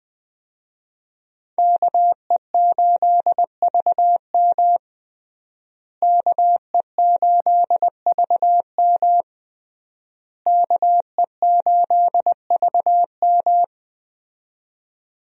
Morse code, heard as a keyed tone, about 21 wpm, 700 Hz. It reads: KE8VM KE8VM KE8VM
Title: USA Call Signs - 5 Characters - Repeated 3x - 20wpm